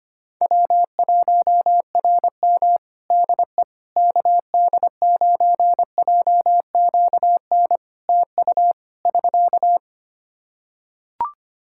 Morse code, heard as a keyed tone, about 25 wpm, 700 Hz. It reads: W1RM DE KD9JQN TU <SK>
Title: K1USN SST QSO Practice 25wpm